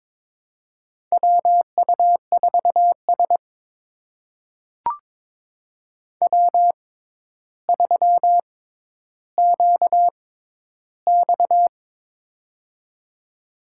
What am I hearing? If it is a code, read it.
Morse code, heard as a keyed tone, about 22 wpm, spaced as for 11 wpm, 700 Hz. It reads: WU4S W3QX
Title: USA Call Signs - 6x Character Spacing - 22wpm